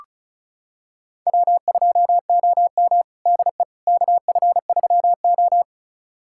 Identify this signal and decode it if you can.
Morse code, heard as a keyed tone, about 36 wpm, 700 Hz. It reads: W2OM DE KF3O
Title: USA Callsign DE Callsign - 4 Characters - 35wpm